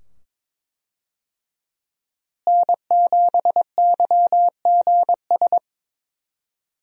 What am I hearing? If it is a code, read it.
Morse code, heard as a keyed tone, about 22 wpm, 700 Hz. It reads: N7YGS
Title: USA Call Signs - 5 Characters - Repeated 2x - 22wpm